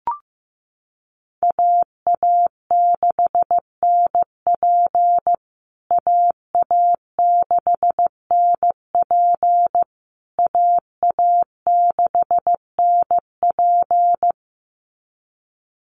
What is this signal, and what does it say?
Morse code, heard as a keyed tone, about 15 wpm, 700 Hz. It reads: AA6NP AA6NP AA6NP
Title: USA Call Signs Repeated 3x 15wpm